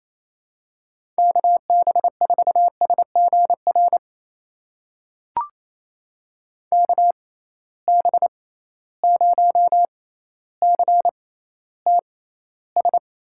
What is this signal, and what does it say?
Morse code, heard as a keyed tone, about 28 wpm, spaced as for 14 wpm, 700 Hz. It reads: KB4SGR KB0CTS
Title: USA Call Signs - 6 Characters - 6x Character Spacing - 28wpm